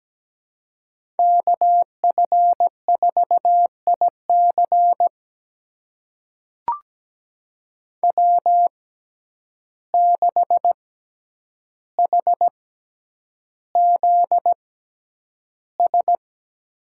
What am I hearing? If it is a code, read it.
Morse code, heard as a keyed tone, about 17 wpm, spaced as for 9 wpm, 700 Hz. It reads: KF4IC W6HZS
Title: USA Call Signs - 5 Characters - 6x Character Spacing - 17wpm